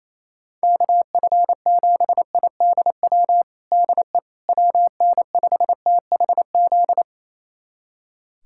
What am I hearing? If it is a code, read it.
Morse code, heard as a keyed tone, about 28 wpm, 700 Hz. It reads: KF7IDW DE WN5THZ
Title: USA Callsign DE Callsign - 6 Characters - 28wpm